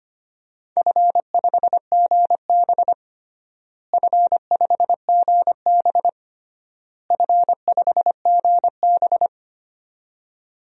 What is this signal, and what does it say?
Morse code, heard as a keyed tone, about 25 wpm, 700 Hz. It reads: F5GB F5GB F5GB
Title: DXCC-100 Call Signs - 4 Characters - Repeated 3x - 25wpm